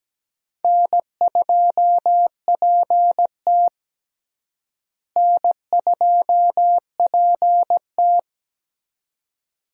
Morse code, heard as a keyed tone, about 17 wpm, 700 Hz. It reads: N2PT N2PT
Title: USA Call Signs - 4 Characters - Repeated 2x - 17wpm